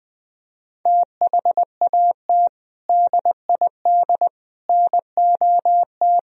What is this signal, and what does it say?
Morse code, heard as a keyed tone, about 21 wpm, 700 Hz. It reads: THAT DID NOT
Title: Sentences from Top 1000 Words - Review Part 2 - 20wpm